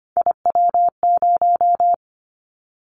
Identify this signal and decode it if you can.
Morse code, heard as a keyed tone, about 25 wpm, 700 Hz. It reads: IW0
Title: Callsign Prefixes for Top 25 DXCC - Repeated 3x - 25wpm